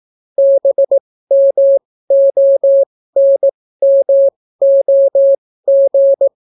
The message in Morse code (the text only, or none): BMONMOG